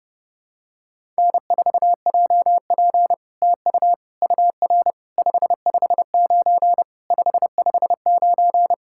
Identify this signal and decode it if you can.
Morse code, heard as a keyed tone, about 30 wpm, 700 Hz. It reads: N4JP TU UR 559 559
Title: SOTA QSO Practice 30wpm